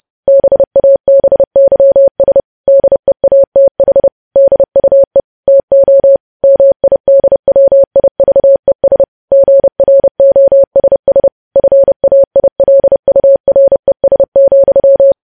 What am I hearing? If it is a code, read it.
Morse code, heard as a keyed tone, about 30 wpm, 550 Hz. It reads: BABYS DEATH DUE TO MIDWIVES GROSS FAILURES,
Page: news headlines in morse code at 30 wpm